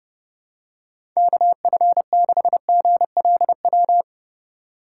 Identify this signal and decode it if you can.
Morse code, heard as a keyed tone, about 32 wpm, 700 Hz. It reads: KF6GLW
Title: USA Call Signs - 6 Characters - 3x Character Spacing - 30wpm